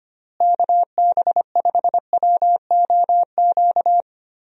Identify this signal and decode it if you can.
Morse code, heard as a keyed tone, about 25 wpm, 700 Hz. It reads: KB5WOQ
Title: USA Call Signs - 6 Characters - Repeated 3x - 25wpm